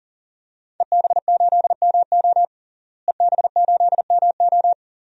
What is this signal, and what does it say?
Morse code, heard as a keyed tone, about 40 wpm, 700 Hz. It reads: EB8MO EB8MO
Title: DXCC-100 Call Signs - 5 Characters - Repeated 2x - 40wpm